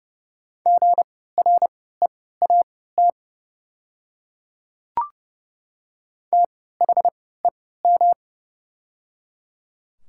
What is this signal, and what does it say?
Morse code, heard as a keyed tone, about 30 wpm, spaced as for 17 wpm, 700 Hz. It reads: GREAT THEM
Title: 3x Character Spacing - Top 200 Words - 30wpm